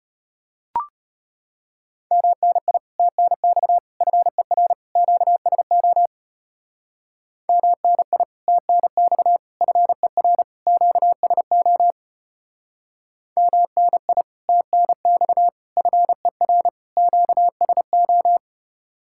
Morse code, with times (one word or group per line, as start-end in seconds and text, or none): 2.1-2.8 MNI
3.0-3.8 TNX
4.0-4.7 FER
5.0-6.1 QSO
7.5-8.2 MNI
8.5-9.4 TNX
9.6-10.4 FER
10.7-11.9 QSO
13.4-14.2 MNI
14.5-15.5 TNX
15.8-16.7 FER
17.0-18.4 QSO